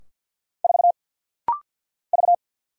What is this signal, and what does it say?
Morse code, heard as a keyed tone, about 50 wpm, 690 Hz. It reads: 4 V
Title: Single Letter-Number - Rapid-Fire 50wpm